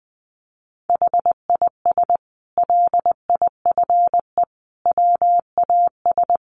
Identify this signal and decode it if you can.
Morse code, heard as a keyed tone, about 20 wpm, 700 Hz. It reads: HIS LIFE WAS
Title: Sentences from Top 1100 Words 20wpm